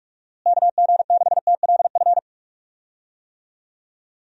Morse code, heard as a keyed tone, about 41 wpm, 700 Hz. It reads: KG6TLF
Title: USA Call Signs - 6 Characters - Repeated 3x - 45wpm